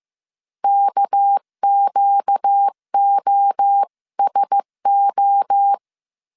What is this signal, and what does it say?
Morse code, heard as a keyed tone, about 15 wpm, 800 Hz. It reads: KQOSO
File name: CW / DET / 1 9 1500C S A 14wpm 3222